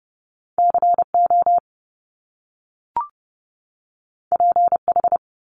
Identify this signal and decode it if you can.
Morse code, heard as a keyed tone, about 30 wpm, 700 Hz. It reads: CO PH
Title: Top Two Letter Combinations in Words 30wpm